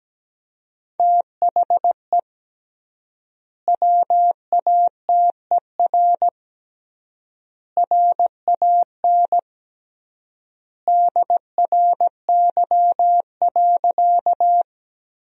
Morse code, heard as a keyed tone, about 17 wpm, 700 Hz. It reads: THE WATER RAN DRY.
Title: 3x Word-Spacing - Sentences from Top 500 Words - 17wpm